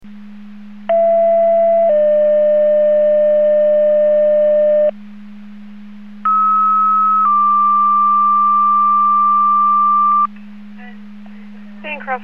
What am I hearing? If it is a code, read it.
Two-tone page, heard as a one-second tone, then a three-second tone, about 670 then 600 Hz, about 1290 then 1180 Hz. It is heard from 0.9 to 10.3 s.